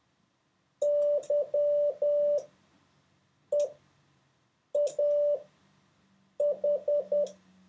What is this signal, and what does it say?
Morse code, heard as a keyed tone, about 10 wpm, spaced as for 6 wpm, 580 Hz. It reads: YEAH
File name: 4of6-words-1